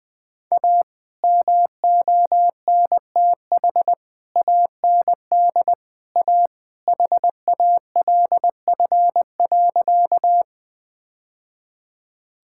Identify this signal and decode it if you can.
Morse code, heard as a keyed tone, about 20 wpm, 700 Hz. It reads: A MONTH AND A HALF.
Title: Sentences from Top 1000 Words - Review Part 5 - 20wpm